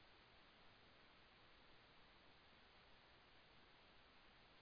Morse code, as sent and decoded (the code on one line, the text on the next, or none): none